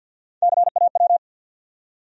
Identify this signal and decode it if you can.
Morse code, heard as a keyed tone, about 53 wpm, 700 Hz. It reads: CAW
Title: USA Call Sign Postfixes - 3 Letters 50wpm